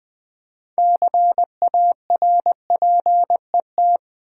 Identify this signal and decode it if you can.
Morse code, heard as a keyed tone, about 20 wpm, 700 Hz. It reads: CARPET
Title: Six Letter Words - 20wpm